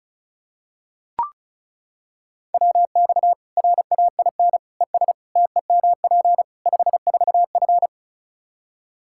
Morse code, heard as a keyed tone, about 36 wpm, 700 Hz. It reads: WX RAIN ES TEMP 54F
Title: Practice QSO - Exchange 2 - Info - 35wpm